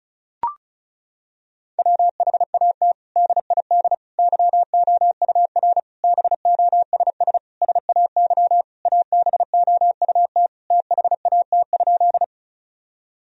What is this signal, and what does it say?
Morse code, heard as a keyed tone, about 34 wpm, 700 Hz. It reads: WHAT DID YOUR BOSS SAY ABOUT THAT?
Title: Sentences from Top 1000 Words - Review Part 4 - 35wpm